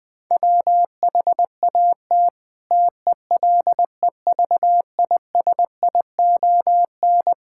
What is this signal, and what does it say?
Morse code, heard as a keyed tone, about 20 wpm, 700 Hz. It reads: WHAT TELEVISION
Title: Sentences from Top 1000 Words - Review Part 5 - 20wpm